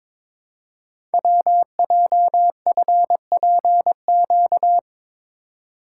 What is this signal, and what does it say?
Morse code, heard as a keyed tone, about 22 wpm, 700 Hz. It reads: WJFPQ
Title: Code Groups - 5 Letters - 22wpm